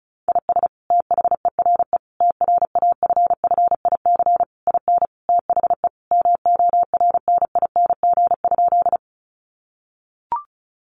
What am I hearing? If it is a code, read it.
Morse code, heard as a keyed tone, about 34 wpm, 700 Hz. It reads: IS THERE TRAFFIC IN THE MORNING?
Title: Sentences from Top 1300 Words 35wpm